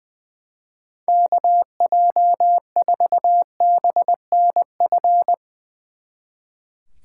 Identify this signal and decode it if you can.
Morse code, heard as a keyed tone, about 21 wpm, 700 Hz. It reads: KJ4BNF
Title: Speed Racing - USA 2x3 Call Signs 20wpm